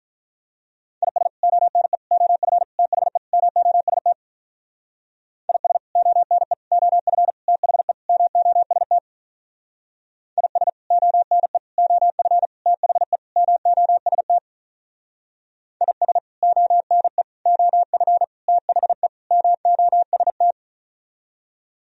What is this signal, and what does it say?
Morse code, heard as a keyed tone, about 45 wpm, 700 Hz. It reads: IS ONE OF THE MOST IS ONE OF THE MOST IS ONE OF THE MOST IS ONE OF THE MOST